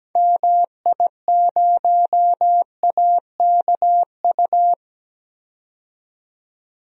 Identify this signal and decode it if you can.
Morse code, heard as a keyed tone, about 17 wpm, 700 Hz. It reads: MI0AKU
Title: DXCC-100 Call Signs - 6 Characters - 17wpm